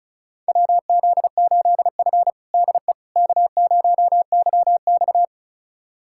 Morse code, heard as a keyed tone, about 36 wpm, 700 Hz. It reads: WZ8F DE K0YX
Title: USA Callsign DE Callsign - 4 Characters - 35wpm